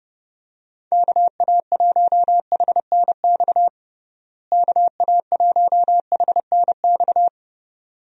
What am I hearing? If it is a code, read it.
Morse code, heard as a keyed tone, about 30 wpm, 700 Hz. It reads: KA1HNX KA1HNX
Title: USA Call Signs - 6 Characters - Repeated 2x - 30wpm